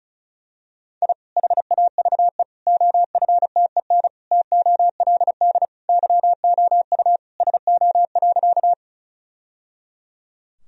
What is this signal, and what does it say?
Morse code, heard as a keyed tone, about 34 wpm, 700 Hz. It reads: I HAVE OFTEN TOLD YOU SO.
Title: Sentences from Top 1000 Words - Review Part 1 - 35wpm